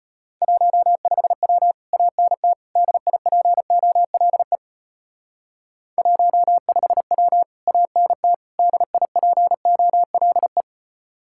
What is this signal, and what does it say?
Morse code, heard as a keyed tone, about 31 wpm, 700 Hz. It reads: 15W ANT DIPOLE 15W ANT DIPOLE
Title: Speed-Racing Common QSO Phrases - 25wpm